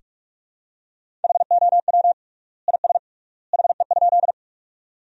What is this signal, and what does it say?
Morse code, heard as a keyed tone, about 46 wpm, 700 Hz. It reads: HOW IS HE?
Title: 3x Word-Spacing - Sentences from Top 100 Words - 45wpm